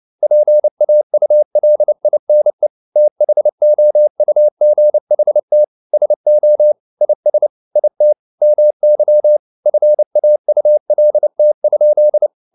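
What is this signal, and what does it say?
Morse code, heard as a keyed tone, about 29 wpm, 600 Hz. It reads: PAULINE THOUGHT SO IS IT MY FAULT?
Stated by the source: Morse code at 29 wpm, sending Ken Follett's novel Never in English, from CW med Gnister.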